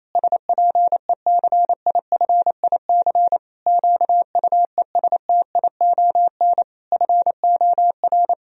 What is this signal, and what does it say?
Morse code, heard as a keyed tone, about 28 wpm, 700 Hz. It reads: SPECIFIC QUESTION FOR
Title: Sentences from Top 1000 Words - Review Part 3 - 28wpm